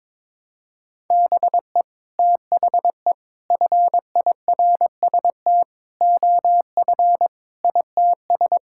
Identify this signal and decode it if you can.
Morse code, heard as a keyed tone, about 22 wpm, 700 Hz. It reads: BE THE FIRST OF ITS
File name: practice / 22wpm / Sets of 8 Words - 22wpm